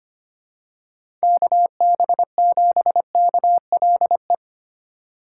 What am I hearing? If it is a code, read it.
Morse code, heard as a keyed tone, about 25 wpm, 700 Hz. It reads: KB7KLE
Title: USA Call Signs - 4x Character Spacing - 25wpm